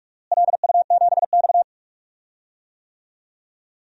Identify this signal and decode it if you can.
Morse code, heard as a keyed tone, about 46 wpm, 700 Hz. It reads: LU7X
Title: DXCC-100 Call Signs - 4 Characters - Repeated 3x - 45wpm